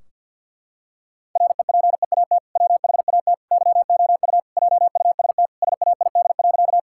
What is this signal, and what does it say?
Morse code, heard as a keyed tone, about 50 wpm, 700 Hz. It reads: REPEAT WHAT YOU JUST SAID.
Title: Sentences from Top 1600 Words 50wpm